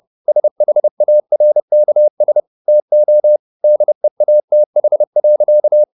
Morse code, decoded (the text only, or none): SHARKS TO DEATH.